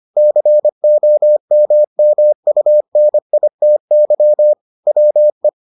Morse code, heard as a keyed tone, about 25 wpm, 600 Hz. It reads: COMMUNITY WE